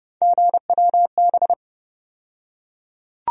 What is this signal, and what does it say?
Morse code, heard as a keyed tone, about 30 wpm, 700 Hz. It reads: GWB
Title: USA Call Sign Postfixes - 3 Letters 30wpm